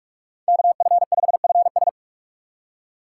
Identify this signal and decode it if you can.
Morse code, heard as a keyed tone, about 46 wpm, 690 Hz. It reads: KF5FS
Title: USA Call Signs - 3x Character Spacing - 45wpm